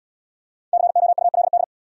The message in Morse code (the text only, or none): COULD